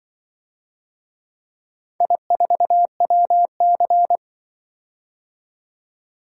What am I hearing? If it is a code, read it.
Morse code, heard as a keyed tone, about 24 wpm, 700 Hz. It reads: I4WC
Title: Speed-Racing - Four Letter-Number Combinations - 20wpm